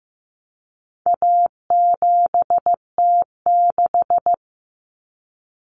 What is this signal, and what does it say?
Morse code, heard as a keyed tone, about 15 wpm, 700 Hz. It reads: A7T6
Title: Common Cut Numbers - 4-Digit 15wpm